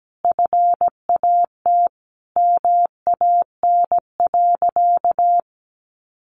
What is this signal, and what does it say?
Morse code, heard as a keyed tone, about 17 wpm, 700 Hz. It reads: FAT MAN.
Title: Sentences from Top 1100 Words 17wpm